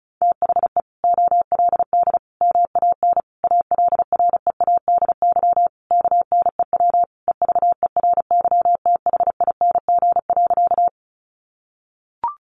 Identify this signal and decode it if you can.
Morse code, heard as a keyed tone, about 34 wpm, 700 Hz. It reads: THE OLD MAN ALREADY KNEW EVERYTHING.
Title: Sentences from Top 700 Words 35wpm